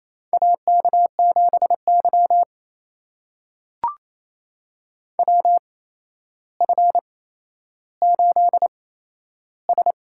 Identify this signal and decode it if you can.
Morse code, heard as a keyed tone, about 28 wpm, spaced as for 12 wpm, 700 Hz. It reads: AK7Y WF8S
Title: USA Call Signs - 4 Characters - 8x Character Spacing - 28wpm